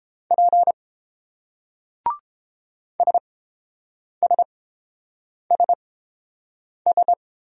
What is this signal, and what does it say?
Morse code, heard as a keyed tone, about 27 wpm, spaced as for 7 wpm, 700 Hz. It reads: P SSSS
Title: Speed-Racing Single Letters 22wpm